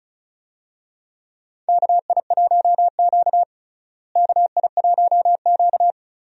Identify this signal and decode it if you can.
Morse code, heard as a keyed tone, about 36 wpm, 700 Hz. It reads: KI1Q KI1Q
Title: USA Call Signs - 4 Characters - Repeated 2x - 35wpm